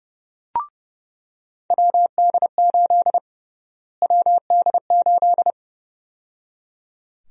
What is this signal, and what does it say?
Morse code, heard as a keyed tone, about 30 wpm, 700 Hz. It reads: WD8 WD8
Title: USA Call Sign Prefixes - Repeated 2x - 30wpm